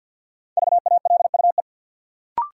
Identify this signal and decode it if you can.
Morse code, heard as a keyed tone, about 48 wpm, 700 Hz. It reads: VALUE
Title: Sentences from Top 600 Words - New Word Drill - 50wpm